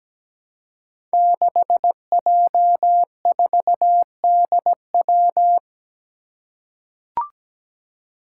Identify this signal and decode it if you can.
Morse code, heard as a keyed tone, about 17 wpm, 700 Hz. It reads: 6J4DW
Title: Five Challenging Letter-Number Combinations 17wpm